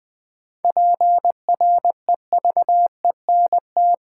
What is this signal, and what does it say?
Morse code, heard as a keyed tone, about 20 wpm, 700 Hz. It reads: PREVENT
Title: Sentences from Top 1100 Words - New Word Drill Alt v2 - 20wpm